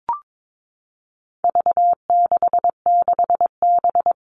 Morse code, heard as a keyed tone, about 22 wpm, 700 Hz. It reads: V66B
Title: All Cut Numbers - 4-Digit 22wpm